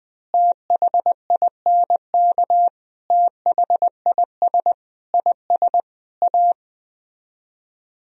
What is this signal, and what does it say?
Morse code, heard as a keyed tone, about 20 wpm, 700 Hz. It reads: THINK THIS IS A
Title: Speed-Racing - Sets of 5 Words 17wpm